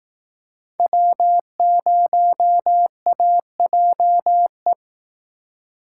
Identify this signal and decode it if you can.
Morse code, heard as a keyed tone, about 18 wpm, 700 Hz. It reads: W0AJE
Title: Speed Racing - USA 1x3 Call Signs 15wpm